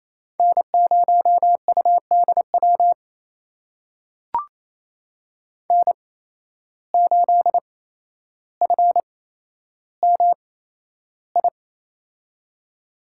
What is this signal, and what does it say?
Morse code, heard as a keyed tone, about 28 wpm, spaced as for 12 wpm, 700 Hz. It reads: N0UDW N8FMI
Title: USA Call Signs - 5 Characters - 8x Character Spacing - 28wpm